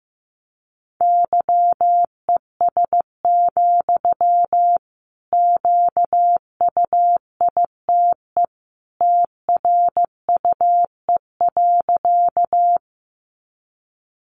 Morse code, heard as a keyed tone, about 15 wpm, 700 Hz. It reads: YES, QUITE TRUE.
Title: Sentences from Top 700 Words 15wpm